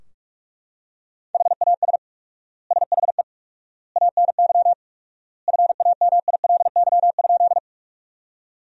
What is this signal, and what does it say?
Morse code, heard as a keyed tone, about 41 wpm, 700 Hz. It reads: HAS SHE ANY FAMILY?
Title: 4x Word-Spacing - Sentences from Top 400 Words - 45wpm